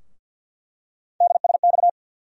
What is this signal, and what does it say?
Morse code, heard as a keyed tone, about 46 wpm, 700 Hz. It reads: DSX
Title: USA Call Sign Postfixes - 3 Letters 50wpm